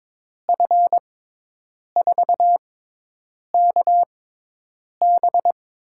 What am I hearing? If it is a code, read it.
Morse code, heard as a keyed tone, about 22 wpm, spaced as for 8 wpm, 700 Hz. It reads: F4KB
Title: USA Call Signs - 6 Characters - 6x Character Spacing - 22wpm